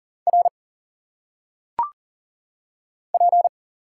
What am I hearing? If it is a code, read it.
Morse code, heard as a keyed tone, about 40 wpm, 700 Hz. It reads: R P
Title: Single Letters - Phonetic Pronunciation 40wpm